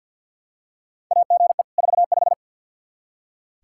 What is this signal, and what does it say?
Morse code, heard as a keyed tone, about 46 wpm, 700 Hz. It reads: AGE 45
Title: Practice QSO - Exchange 3 - More Info - 50wpm